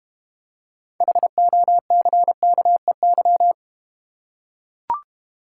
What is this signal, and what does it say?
Morse code, heard as a keyed tone, about 31 wpm, 700 Hz. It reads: HOCKEY
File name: Easy Words - Speed-Racing with Hints - 25wpm